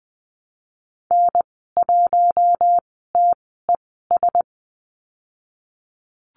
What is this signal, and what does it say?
Morse code, heard as a keyed tone, about 20 wpm, spaced as for 14 wpm, 700 Hz. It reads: N1TES
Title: 055 Lesson 10g - Letter r - Call Signs 20wpm